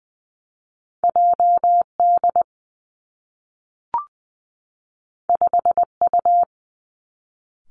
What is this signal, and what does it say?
Morse code, heard as a keyed tone, about 20 wpm, 700 Hz. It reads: J D 5 U
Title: Two Challenging Letter-Number Combinations 20wpm